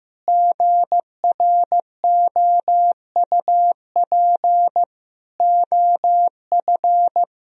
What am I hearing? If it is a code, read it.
Morse code, heard as a keyed tone, about 15 wpm, 700 Hz. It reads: GROUP OF
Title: Sentences from Top 1000 Words - Review Part 1 - 15wpm